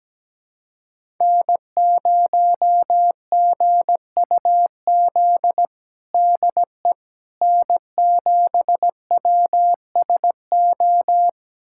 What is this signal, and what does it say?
Morse code, heard as a keyed tone, about 17 wpm, 700 Hz. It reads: N0GUZ DE N7WSO
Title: USA Callsign DE Callsign - 5 Characters - 17wpm